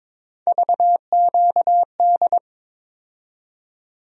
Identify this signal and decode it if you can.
Morse code, heard as a keyed tone, about 22 wpm, 700 Hz. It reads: VQD
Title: USA Call Sign Postfixes - 3 Letters 22wpm